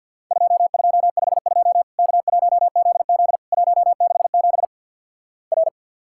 Morse code, heard as a keyed tone, about 50 wpm, 700 Hz. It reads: 2252 K1ZZ 167 R
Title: Worked All Europe DX Contest - QTC Exchange with Repeats - 50wpm